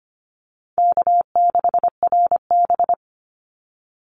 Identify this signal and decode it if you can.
Morse code, heard as a keyed tone, about 25 wpm, 700 Hz. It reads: K6RB
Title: Call Signs Repeated 2x - 25wpm